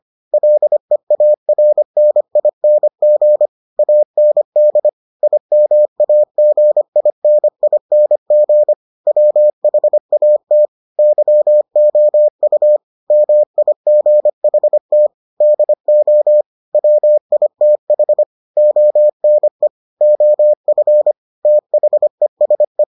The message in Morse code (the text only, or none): LEARNING AND IMAGINING WHAT YOU MIGHT DO WITH ONE OF THESE